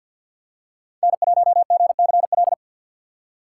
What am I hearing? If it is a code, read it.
Morse code, heard as a keyed tone, about 51 wpm, 700 Hz. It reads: N1GCL
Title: USA Call Signs - 5 Characters - 2x Character Spacing - 50wpm